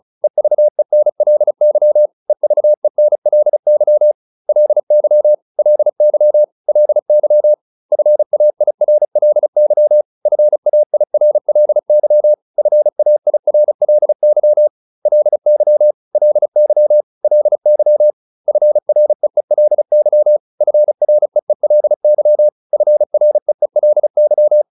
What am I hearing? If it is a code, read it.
Morse code, heard as a keyed tone, about 35 wpm, 600 Hz. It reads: EVENLY EVENLY LY LY LY FAIRLY FAIRLY FAIRLY LY LY LY FREELY FREELY FREELY